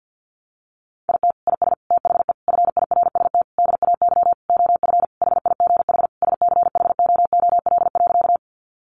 Morse code, heard as a keyed tone, about 50 wpm, 700 Hz. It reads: IT IS THE FIRST DAY OF HIGH SCHOOL.
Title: Additional Easy Phrases 50wpm